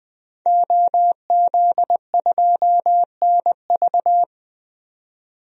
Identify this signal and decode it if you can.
Morse code, heard as a keyed tone, about 21 wpm, 700 Hz. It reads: OZ2NV
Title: DXCC-100 Call Signs - 5 Characters - 20wpm